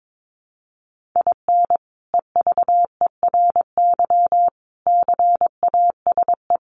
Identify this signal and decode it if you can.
Morse code, heard as a keyed tone, about 22 wpm, 700 Hz. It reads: IN EVERY CASE
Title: Sentences from Top 1200 Words 22wpm